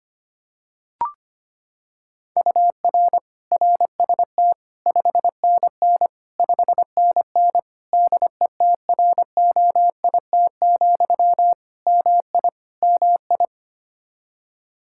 Morse code, heard as a keyed tone, about 25 wpm, 700 Hz. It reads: UR RST 5NN 5NN DETROIT, MI MI
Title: Practice QSO - Exchange 1 - Beginning - 25wpm